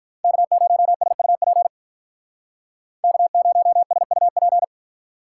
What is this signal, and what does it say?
Morse code, heard as a keyed tone, about 45 wpm, 700 Hz. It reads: K0SUP K0SUP